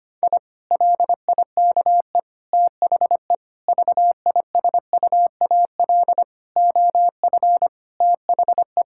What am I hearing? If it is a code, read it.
Morse code, heard as a keyed tone, about 25 wpm, 700 Hz. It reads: I LIKE THE VISUAL OF THE
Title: Sentences from Top 1000 Words - Review Part 6 - 25wpm